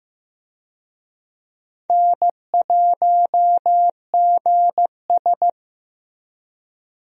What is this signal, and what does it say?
Morse code, heard as a keyed tone, about 16 wpm, 700 Hz. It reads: N1GS
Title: USA Call Signs - 4 Characters - Repeated 2x - 15wpm